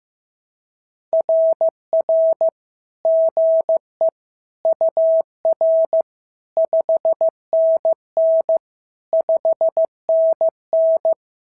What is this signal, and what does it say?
Morse code, heard as a keyed tone, about 14 wpm, 650 Hz. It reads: RR GE UR 5NN 5NN
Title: SOTA QSO Practice 15wpm